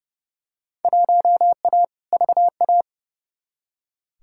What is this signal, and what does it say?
Morse code, heard as a keyed tone, about 30 wpm, 700 Hz. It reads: 1A VA
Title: Field Day Exchanges - 30wpm